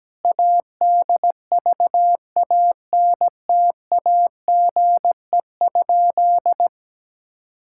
Morse code, heard as a keyed tone, about 17 wpm, 700 Hz. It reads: ADVANTAGE?
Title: Sentences from Top 1000 Words - Review Part 6 - 17wpm